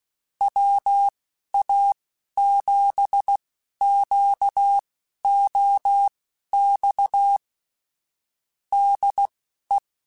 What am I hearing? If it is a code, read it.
Morse code, heard as a keyed tone, about 16 wpm, spaced as for 11 wpm, 800 Hz. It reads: WA7QOX DE
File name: QSOs / QSOs-mp3s / WA7QOX 1@10